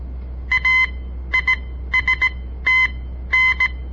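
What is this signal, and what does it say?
Morse code, heard as a keyed tone, about 17 wpm, spaced as for 12 wpm, 2050 Hz. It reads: AISTN
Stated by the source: On-air Morse code practice broadcast by a southern Montana amateur radio association; all code, no read back.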